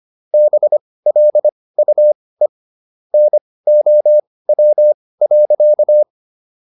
Morse code, heard as a keyed tone, about 25 wpm, spaced as for 18 wpm, 600 Hz. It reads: BLUE NOW.